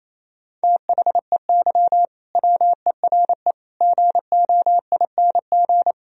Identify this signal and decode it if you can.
Morse code, heard as a keyed tone, about 28 wpm, 700 Hz. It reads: THEY WERE GOING
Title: Sentences from Top 1000 Words - Review Part 4 - 28wpm